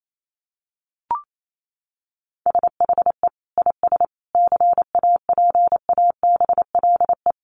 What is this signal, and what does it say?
Morse code, heard as a keyed tone, about 28 wpm, 700 Hz. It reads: SHE IS CAPABLE
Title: Sentences from Top 1200 Words 28wpm